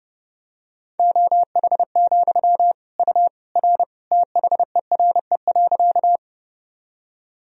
Morse code, heard as a keyed tone, about 30 wpm, 700 Hz. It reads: OH, U R THERE.
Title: Sentences from Top 500 Words and Abbreviations 30wpm